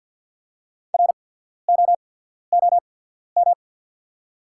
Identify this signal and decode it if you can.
Morse code, heard as a keyed tone, about 48 wpm, spaced as for 13 wpm, 690 Hz. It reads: ROOM
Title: 8x Character Spacing - Top 300 Words - 50wpm